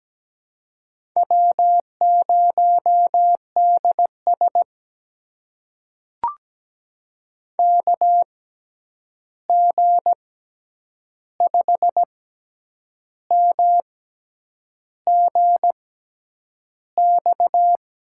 Morse code, heard as a keyed tone, about 17 wpm, spaced as for 8 wpm, 700 Hz. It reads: W0DS KG5MGX
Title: USA Call Signs - 6x Character Spacing - 17wpm